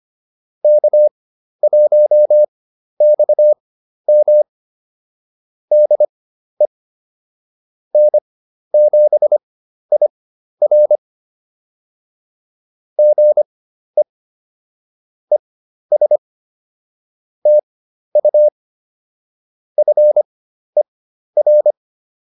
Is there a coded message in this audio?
Yes: Morse code from 0.6 to 21.7 s, heard as a keyed tone, about 25 wpm, spaced as for 12 wpm, 600 Hz. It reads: K1XM DE N7IR GE ES TU FER